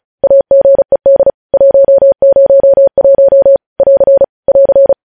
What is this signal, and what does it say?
Morse code, heard as a keyed tone, about 36 wpm, 550 Hz. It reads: AGED 101 <AR> <AR>
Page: news headlines in morse code at 35 wpm